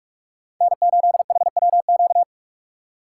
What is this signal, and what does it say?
Morse code, heard as a keyed tone, about 44 wpm, 700 Hz. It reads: N8HWQ